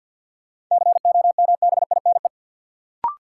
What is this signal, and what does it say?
Morse code, heard as a keyed tone, about 53 wpm, 700 Hz. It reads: COMBINE
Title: Sentences from Top 1600 Words - New Word Drill Alt - 50wpm